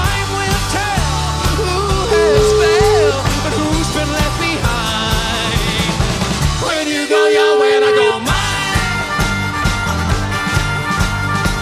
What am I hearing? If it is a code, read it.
Ringback tone (Europe, 425 Hz), with noise.